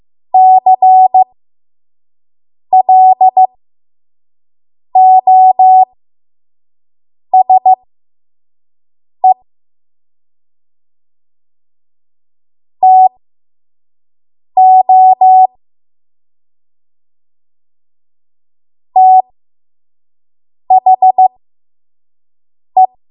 Morse code, heard as a keyed tone, about 15 wpm, spaced as for 5 wpm, 750 Hz. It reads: CLOSE TO THE
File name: Morse / 05WPM